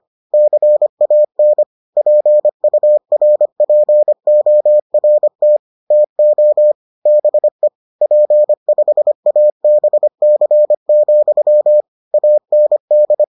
Morse code, heard as a keyed tone, about 25 wpm, 600 Hz. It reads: CAN PURPORT TO BE P5ABC, AND